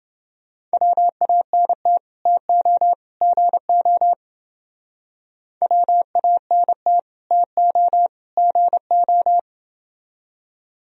Morse code, heard as a keyed tone, about 28 wpm, 700 Hz. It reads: WANT TO GO WANT TO GO